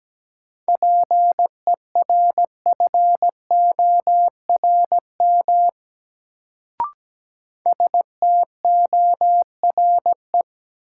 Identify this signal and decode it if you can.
Morse code, heard as a keyed tone, about 17 wpm, 700 Hz. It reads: PERFORM STORE
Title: Sentences from Top 1000 Words - Review Part 5 - New Word Drill - 17wpm